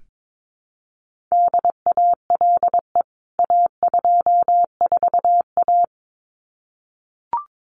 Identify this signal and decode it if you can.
Morse code, heard as a keyed tone, about 22 wpm, 700 Hz. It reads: DALE A24A
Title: CWT Exchange 22wpm